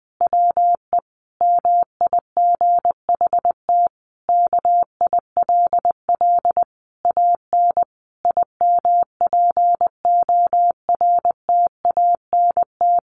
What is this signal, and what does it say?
Morse code, heard as a keyed tone, about 20 wpm, 700 Hz. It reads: WE MIGHT KILL AN IMPORTANT
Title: A Sound Of Thunder 20wpm